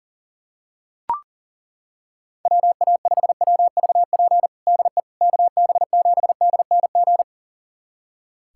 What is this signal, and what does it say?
Morse code, heard as a keyed tone, about 37 wpm, 700 Hz. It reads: WA5WVP DE KB7DNG
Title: USA Callsign DE Callsign - 6 Characters - 40wpm